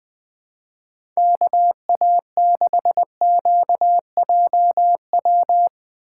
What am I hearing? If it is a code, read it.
Morse code, heard as a keyed tone, about 21 wpm, 700 Hz. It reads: KA6QJW
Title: USA Call Signs - 6 Characters - 2x Character Spacing - 20wpm